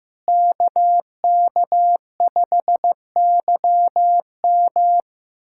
Morse code, heard as a keyed tone, about 15 wpm, 700 Hz. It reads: KK5YM
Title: USA Callsign DE Callsign - 5 Characters - 15wpm